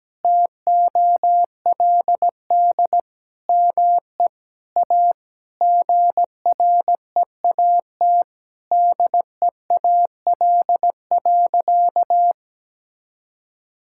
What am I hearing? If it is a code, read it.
Morse code, heard as a keyed tone, about 17 wpm, 700 Hz. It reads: TOLD ME A GREAT DEAL.
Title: Sentences from Top 1000 Words - Review Part 6 - 17wpm